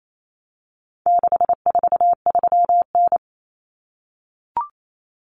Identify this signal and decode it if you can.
Morse code, heard as a keyed tone, about 28 wpm, 700 Hz. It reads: 643N
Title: Common Cut Numbers - 4-Digit 28wpm